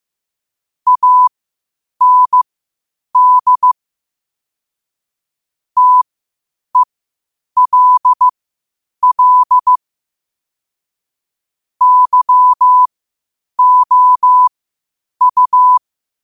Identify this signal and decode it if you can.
Morse code, heard as a keyed tone, about 15 wpm, spaced as for 8 wpm, 1000 Hz. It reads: AND TELL YOU
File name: qotdfiles / 8wpm